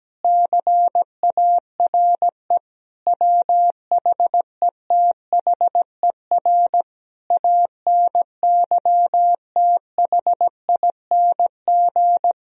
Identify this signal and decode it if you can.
Morse code, heard as a keyed tone, about 17 wpm, 700 Hz. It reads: CARE WHETHER ANYTHING
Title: Sentences from Top 1000 Words - Review Part 3 - 17wpm